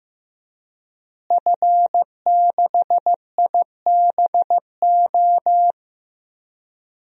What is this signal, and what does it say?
Morse code, heard as a keyed tone, about 15 wpm, 700 Hz. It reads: F6IBO
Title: DXCC-100 Call Signs - 5 Characters - Repeated 2x - 15wpm